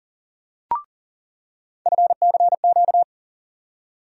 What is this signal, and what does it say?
Morse code, heard as a keyed tone, about 40 wpm, 700 Hz. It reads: FCQ
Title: USA Call Sign Postfixes - 3 Letters 40wpm